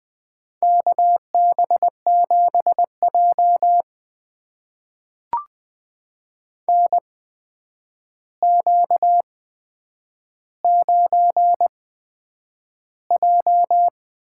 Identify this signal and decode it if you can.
Morse code, heard as a keyed tone, about 20 wpm, spaced as for 9 wpm, 700 Hz. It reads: KB7J NQ9J
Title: USA Call Signs - 4 Characters - 8x Character Spacing - 20wpm